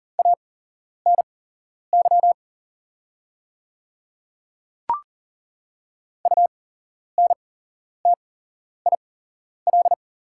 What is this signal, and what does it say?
Morse code, heard as a keyed tone, about 40 wpm, spaced as for 11 wpm, 700 Hz. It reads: ANY UNTIL
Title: 8x Character Spacing - Top 300 Words - 40wpm